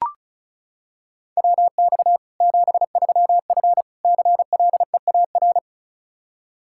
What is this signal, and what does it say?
Morse code, heard as a keyed tone, about 36 wpm, 700 Hz. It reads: WX 73F CLEAR
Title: Practice QSO - Exchange 2 - Info - 35wpm